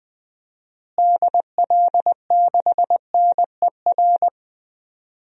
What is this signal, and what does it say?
Morse code, heard as a keyed tone, about 20 wpm, 700 Hz. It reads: DL6NER
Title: DXCC-100 Call Signs - 6 Characters - Repeated 3x - 20wpm